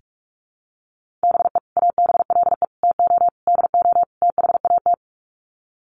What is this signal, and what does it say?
Morse code, heard as a keyed tone, about 44 wpm, 700 Hz. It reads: BE ABLE TO DO THAT
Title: Sets of 5 Words Encore2 45wpm